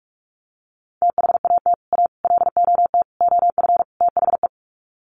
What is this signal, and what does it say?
Morse code, heard as a keyed tone, about 44 wpm, 700 Hz. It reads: THAT A LOT OF THE
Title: Sets of 5 Words Encore2 45wpm